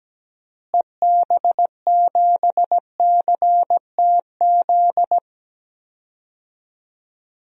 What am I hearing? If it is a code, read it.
Morse code, heard as a keyed tone, about 17 wpm, 700 Hz. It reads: EB7CTZ
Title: DXCC-100 Call Signs - 6 Characters - Repeated 3x - 17wpm